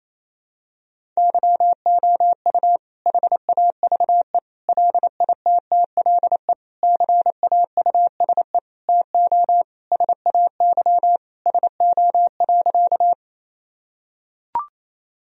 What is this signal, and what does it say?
Morse code, heard as a keyed tone, about 28 wpm, 700 Hz. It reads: YOU HAVE LITTLE CAUSE TO SAY SO.
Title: Sentences from Top 1000 Words - Review Part 1 - 28wpm